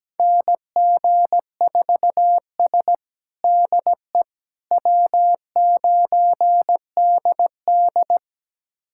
Morse code, heard as a keyed tone, about 17 wpm, 700 Hz. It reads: NG4S DE W9DD
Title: USA Callsign DE Callsign - 4 Characters - 17wpm